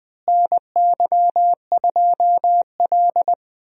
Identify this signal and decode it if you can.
Morse code, heard as a keyed tone, about 20 wpm, 700 Hz. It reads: NY2L
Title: USA Call Signs - 4 Characters - Repeated 3x - 20wpm